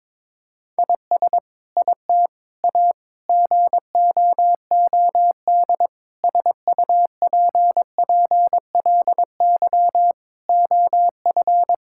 Morse code, heard as a keyed tone, about 22 wpm, 700 Hz. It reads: IS IT A GOOD SUPPLY OF
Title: Sentences from Top 1000 Words - Review Part 5 - 22wpm